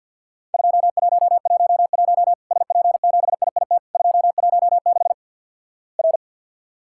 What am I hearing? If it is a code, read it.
Morse code, heard as a keyed tone, about 48 wpm, 700 Hz. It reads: 2111 SP7IIT 216 R
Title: Worked All Europe DX Contest - QTC Exchange - 50wpm